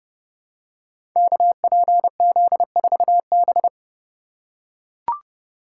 Morse code, heard as a keyed tone, about 30 wpm, 700 Hz. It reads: KPZ4B
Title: Code Groups - 5 Letter:Numbers - 30wpm